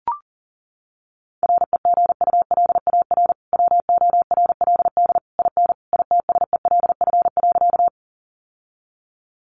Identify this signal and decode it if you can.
Morse code, heard as a keyed tone, about 37 wpm, 700 Hz. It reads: REGULAR WORLD IN ITSELF.
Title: Sentences from Top 900 Words 40wpm